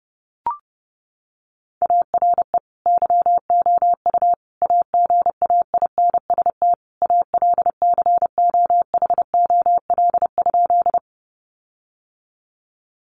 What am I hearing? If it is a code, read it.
Morse code, heard as a keyed tone, about 30 wpm, 700 Hz. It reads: ARE YOU AGAINST ALCOHOL?
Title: Sentences from Top 1400 Words 30wpm